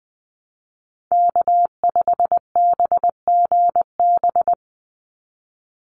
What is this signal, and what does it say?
Morse code, heard as a keyed tone, about 20 wpm, 700 Hz. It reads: K5BGB
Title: Call Signs - 20wpm